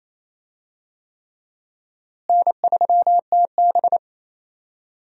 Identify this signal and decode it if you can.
Morse code, heard as a keyed tone, about 28 wpm, 700 Hz. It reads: N3TB
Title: USA Call Signs - 4 Characters - Repeated 3x - 28wpm